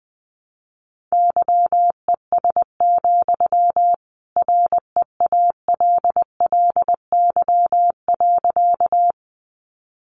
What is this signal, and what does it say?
Morse code, heard as a keyed tone, about 20 wpm, 700 Hz. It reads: YES, REALLY.
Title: Sentences from Top 600 Words 20wpm